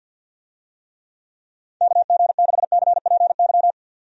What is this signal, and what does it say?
Morse code, heard as a keyed tone, about 52 wpm, 700 Hz. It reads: KG6CPY
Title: USA Call Signs - 6 Characters - 50wpm